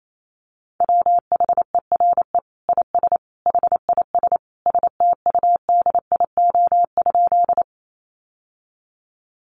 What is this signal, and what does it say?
Morse code, heard as a keyed tone, about 28 wpm, 700 Hz. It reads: WHERE IS HIS STUDIO?
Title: Sentences from Top 1200 Words 28wpm